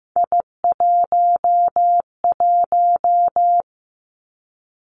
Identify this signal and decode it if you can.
Morse code, heard as a keyed tone, about 15 wpm, 700 Hz. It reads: I11
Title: Three Challenging Letter-Number Combinations 15wpm